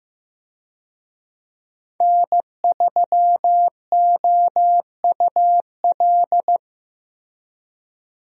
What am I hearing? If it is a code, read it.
Morse code, heard as a keyed tone, about 16 wpm, 700 Hz. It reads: N3OUL
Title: USA Call Signs - 5 Characters - Repeated 3x - 15wpm